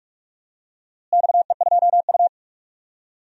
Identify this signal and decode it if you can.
Morse code, heard as a keyed tone, about 43 wpm, 700 Hz. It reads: XE2U
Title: DXCC-100 Call Signs - 4 Characters - Repeated 3x - 45wpm